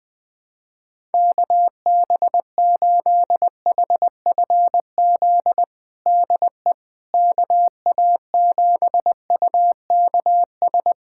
Morse code, heard as a keyed tone, about 21 wpm, 700 Hz. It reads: KB8HFZ DE KA7UKS
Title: USA Callsign DE Callsign - 6 Characters - 20wpm